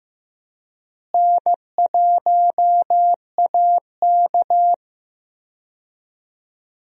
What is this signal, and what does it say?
Morse code, heard as a keyed tone, about 15 wpm, 700 Hz. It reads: N1AK
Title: USA Call Signs - 4 Characters - Repeated 3x - 15wpm